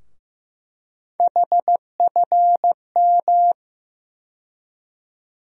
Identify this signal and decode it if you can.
Morse code, heard as a keyed tone, about 15 wpm, 700 Hz. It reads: HFM
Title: USA Call Sign Postfixes - 3 Letters 15wpm